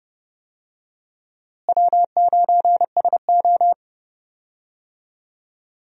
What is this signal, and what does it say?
Morse code, heard as a keyed tone, about 29 wpm, 700 Hz. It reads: W9SO
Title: USA Call Signs - 4 Characters - Repeated 3x - 30wpm